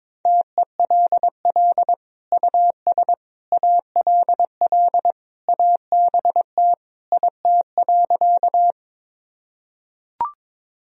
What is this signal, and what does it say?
Morse code, heard as a keyed tone, about 23 wpm, 700 Hz. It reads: TELL US ALL ABT IT.
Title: Sentences from Top 200 Words and Abbreviations 22wpm